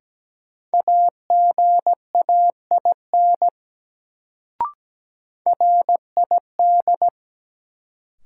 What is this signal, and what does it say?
Morse code, heard as a keyed tone, about 17 wpm, 700 Hz. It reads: AGAIN RID
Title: Sets of 4 Words - New Word Drill - 17wpm